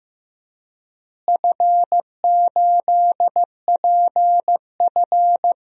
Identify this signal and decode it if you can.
Morse code, heard as a keyed tone, about 16 wpm, 700 Hz. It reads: F8PF
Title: DXCC-100 Call Signs - 4 Characters - Repeated 2x - 15wpm